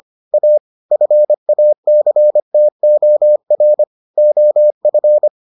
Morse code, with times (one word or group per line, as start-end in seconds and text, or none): 0.3-0.6 A
0.9-3.8 FACTOR
4.2-5.3 OF